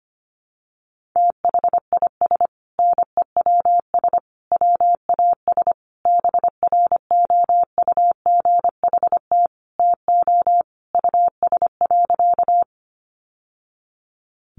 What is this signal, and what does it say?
Morse code, heard as a keyed tone, about 25 wpm, 700 Hz. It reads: THIS NEWS WAS BROUGHT TO US.
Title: Sentences from Top 800 Words 25wpm